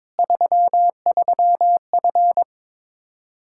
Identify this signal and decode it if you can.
Morse code, heard as a keyed tone, about 22 wpm, 700 Hz. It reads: 33F
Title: Practice QSO - Exchange 2 - Info - 22wpm